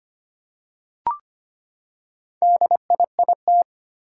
Morse code, heard as a keyed tone, about 25 wpm, 700 Hz. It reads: DIIT
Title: Four Challenging Letter-Number Combinations 25wpm